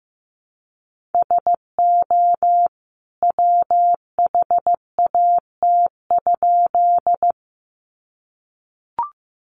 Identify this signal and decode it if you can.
Morse code, heard as a keyed tone, about 15 wpm, 700 Hz. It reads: SO WHAT?
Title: A Sound Of Thunder 15wpm